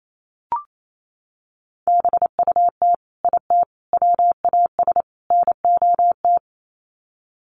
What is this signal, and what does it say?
Morse code, heard as a keyed tone, about 28 wpm, 700 Hz. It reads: BUT IT WAS NOT
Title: Sets of 4 Words 28wpm